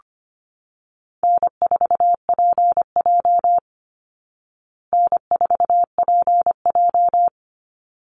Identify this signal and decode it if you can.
Morse code, heard as a keyed tone, about 25 wpm, 700 Hz. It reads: N4PJ N4PJ
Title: Easy Call Signs Repeated 2x - 25wpm